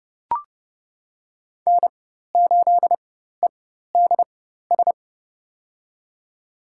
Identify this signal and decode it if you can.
Morse code, heard as a keyed tone, about 29 wpm, spaced as for 14 wpm, 700 Hz. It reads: N8EDS
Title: USA Call Signs - 4x Character Spacing - 30wpm